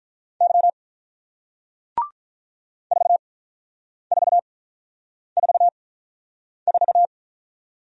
Morse code, heard as a keyed tone, about 48 wpm, spaced as for 9 wpm, 700 Hz. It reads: Y 4444